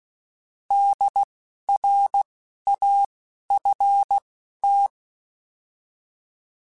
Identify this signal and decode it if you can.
Morse code, heard as a keyed tone, about 16 wpm, spaced as for 12 wpm, 800 Hz. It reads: DRAFT
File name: Alphabetical / Alphabetical mp3s / Ds @10